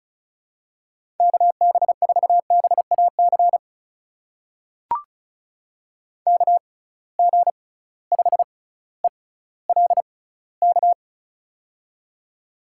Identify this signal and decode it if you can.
Morse code, heard as a keyed tone, about 36 wpm, spaced as for 18 wpm, 700 Hz. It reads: KB4BAC KG5ELK
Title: USA Call Signs - 6 Characters - 6x Character Spacing - 35wpm